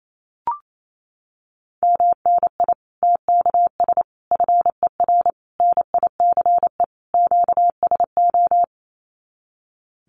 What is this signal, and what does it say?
Morse code, heard as a keyed tone, about 28 wpm, 700 Hz. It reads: MNI TKS FER NICE QSO
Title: Common QSO Phrases 28wpm